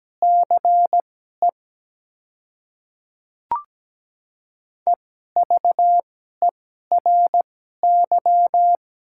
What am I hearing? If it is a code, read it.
Morse code, heard as a keyed tone, about 17 wpm, spaced as for 12 wpm, 690 Hz. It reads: CE EVERY
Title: 2x Character Spacing - Top 300 Words - 17wpm